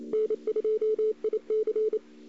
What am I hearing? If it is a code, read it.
Morse code, heard as a keyed tone, about 28 wpm, 440 Hz. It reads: N2IC
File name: CW / 2008-ss-cw-k5tr-op-n5rz -02:00